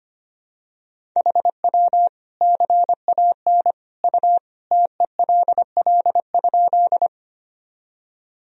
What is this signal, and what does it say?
Morse code, heard as a keyed tone, about 25 wpm, 700 Hz. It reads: HW CAN U TELL?
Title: Sentences from Top 200 Words and Abbreviations 25wpm